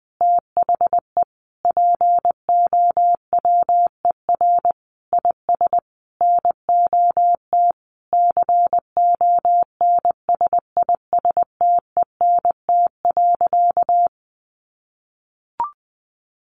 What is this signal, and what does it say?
Morse code, heard as a keyed tone, about 20 wpm, 700 Hz. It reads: THE POWER IS NOT CONSISTENT.
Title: Sentences from Top 1500 Words 20wpm